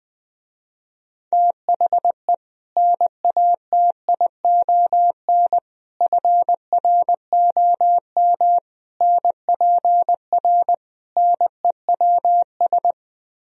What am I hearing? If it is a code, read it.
Morse code, heard as a keyed tone, about 20 wpm, 700 Hz. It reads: THE NATION FROM NPR NEWS
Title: Speed-Racing - Sets of 5 Words 17wpm